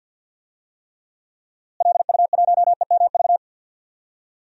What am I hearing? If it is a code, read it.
Morse code, heard as a keyed tone, about 52 wpm, 700 Hz. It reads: LU1EMV
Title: DXCC-100 Call Signs - 6 Characters - Repeated 2x - 50wpm